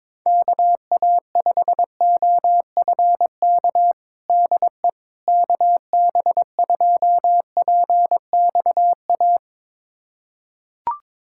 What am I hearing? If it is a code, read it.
Morse code, heard as a keyed tone, about 22 wpm, 700 Hz. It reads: KA5OFK DE KB2PXA
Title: USA Callsign DE Callsign - 6 Characters - 22wpm